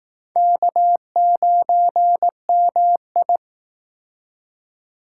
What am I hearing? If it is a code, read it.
Morse code, heard as a keyed tone, about 19 wpm, 700 Hz. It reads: K9MI
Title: Speed Racing - USA 1x2 Call Signs 15wpm